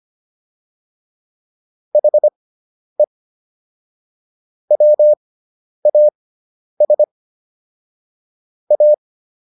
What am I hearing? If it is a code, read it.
Morse code, heard as a keyed tone, about 25 wpm, spaced as for 10 wpm, 600 Hz. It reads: HE WAS A